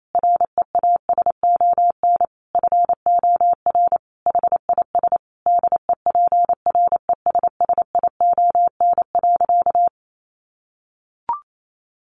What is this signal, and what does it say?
Morse code, heard as a keyed tone, about 28 wpm, 700 Hz. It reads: REASON FOR HIS DEPRESSION.
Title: Sentences from Top 1300 Words 28wpm